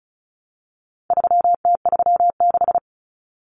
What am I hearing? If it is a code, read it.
Morse code, heard as a keyed tone, about 34 wpm, 700 Hz. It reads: 3T36
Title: Common Cut Numbers - 4-Digit 35wpm